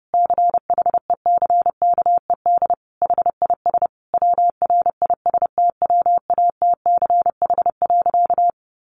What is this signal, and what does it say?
Morse code, heard as a keyed tone, about 28 wpm, 700 Hz. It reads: CHECKED HIS WRISTWATCH.
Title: A Sound Of Thunder 30wpm